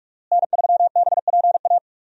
Morse code, heard as a keyed tone, about 46 wpm, 700 Hz. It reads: N3BPA
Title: USA Call Signs - 5 Characters - Repeated 2x - 45wpm